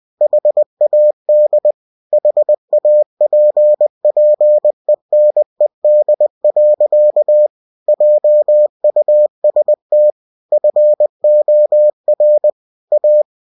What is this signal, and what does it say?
Morse code, heard as a keyed tone, about 20 wpm, 600 Hz. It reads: HAD HAPPENED. JUST FOR A
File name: cw-books / winnie-the-pooh / winnie-20wpm-page-0071